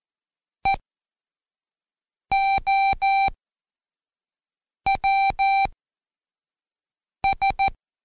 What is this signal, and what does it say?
Morse code, heard as a keyed tone, about 14 wpm, spaced as for 5 wpm, 780 Hz. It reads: EOWS
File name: Code Quick / AudioDownld1 / Seg18-Day2-Quiz ASYO